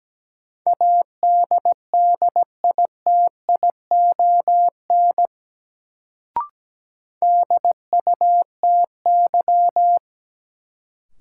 Morse code, heard as a keyed tone, about 17 wpm, 700 Hz. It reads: ADDITION DUTY
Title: Sentences from Top 1500 Words - New Word Drill Alt - 17wpm